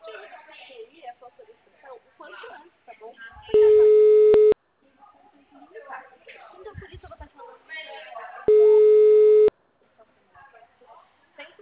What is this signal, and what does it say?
Ringback tone (Europe, 425 Hz), with speech.